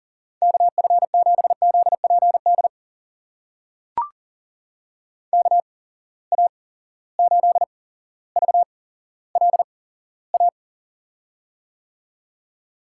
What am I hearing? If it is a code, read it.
Morse code, heard as a keyed tone, about 43 wpm, spaced as for 17 wpm, 700 Hz. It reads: KF7ZPD KA8VLA
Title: USA Call Signs - 6 Characters - 8x Character Spacing - 40wpm